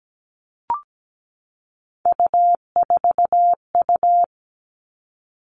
Morse code, heard as a keyed tone, about 17 wpm, 700 Hz. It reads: U4U
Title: All Cut Numbers - 3-Digit 17wpm